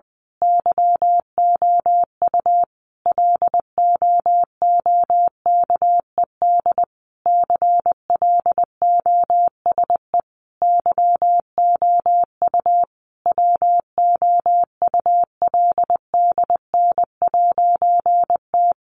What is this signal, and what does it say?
Morse code, heard as a keyed tone, about 20 wpm, 700 Hz. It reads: YOU LOOKED CLOSE YOU WOULDN'T
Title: A Sound Of Thunder 20wpm